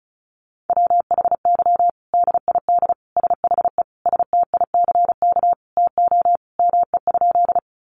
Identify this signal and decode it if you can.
Morse code, heard as a keyed tone, about 34 wpm, 700 Hz. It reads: WHY DID SHE STICK TO ME?
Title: Sentences from Top 1200 Words 35wpm